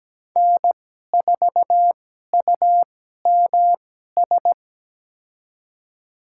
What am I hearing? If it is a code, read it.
Morse code, heard as a keyed tone, about 17 wpm, spaced as for 12 wpm, 700 Hz. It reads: N4UMS
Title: USA Call Signs - 5 Characters - 2x Character Spacing - 17wpm